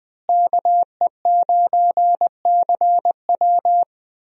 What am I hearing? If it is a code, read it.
Morse code, heard as a keyed tone, about 20 wpm, 700 Hz. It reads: KE9CW
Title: USA Call Signs - 5 Characters - Repeated 3x - 20wpm